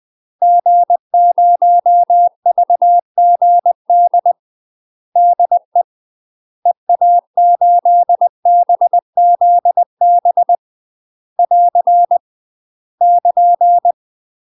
Morse code, heard as a keyed tone, about 20 wpm, 700 Hz. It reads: G0VGD DE EA8BZB <AR> <KN>
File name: UK MASTER QSOs TRAINING / 20WPM / UK MASTER QSOs 0034